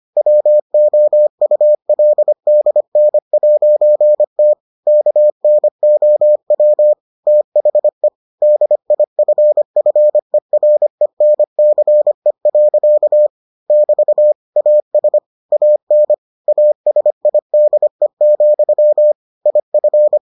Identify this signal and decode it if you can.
Morse code, heard as a keyed tone, about 25 wpm, 600 Hz. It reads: WOULDN'T KNOW THE DIFFERENCE. <BT> AS AN ASIDE, IF